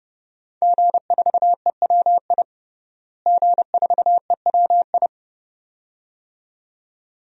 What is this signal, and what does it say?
Morse code, heard as a keyed tone, about 29 wpm, 700 Hz. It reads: G4EWI G4EWI
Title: DXCC-100 Call Signs - 5 Characters - Repeated 2x - 30wpm